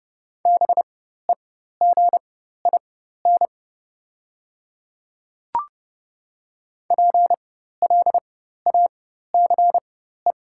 Morse code, heard as a keyed tone, about 29 wpm, spaced as for 14 wpm, 690 Hz. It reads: BEGIN PLACE
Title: 4x Character Spacing - Top 500 Words - 30wpm